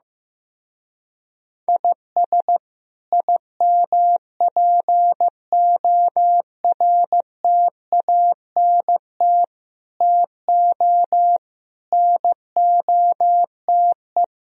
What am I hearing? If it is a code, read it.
Morse code, heard as a keyed tone, about 15 wpm, 700 Hz. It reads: IS IMPORTANT TO NOTE
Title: Speed-Racing - Sets of 5 Words 15wpm